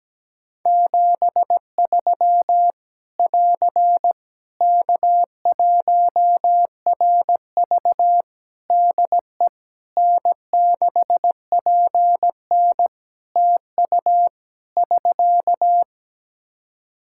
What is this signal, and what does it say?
Morse code, heard as a keyed tone, about 17 wpm, 700 Hz. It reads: 73 <AR> K1RV DE N6PN TU <SK>
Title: Practice QSO - Exchange 4 - Ending - 17wpm